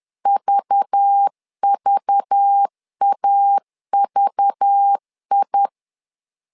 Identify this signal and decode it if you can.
Morse code, heard as a keyed tone, about 10 wpm, 800 Hz. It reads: VVAVI
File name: LEZIONI / 5 LBVC / 5 LBV 10wpm 2248